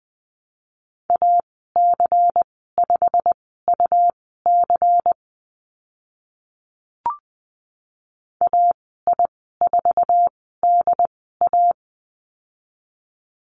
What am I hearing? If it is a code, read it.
Morse code, heard as a keyed tone, about 20 wpm, spaced as for 15 wpm, 700 Hz. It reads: AC5UC AI4DA
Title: 122 Lesson 20g - Number 3 - Call Signs 20wpm